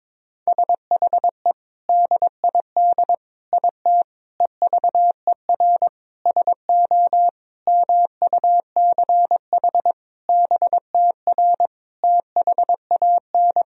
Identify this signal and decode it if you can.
Morse code, heard as a keyed tone, about 22 wpm, 700 Hz. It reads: SHE DID IT EVER SO MUCH BTR THAN
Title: Sentences from Top 400 Words and Abbreviations 22wpm